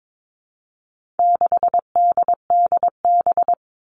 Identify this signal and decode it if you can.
Morse code, heard as a keyed tone, about 22 wpm, 700 Hz. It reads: 6DDB
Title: All Cut Numbers - 4-Digit 22wpm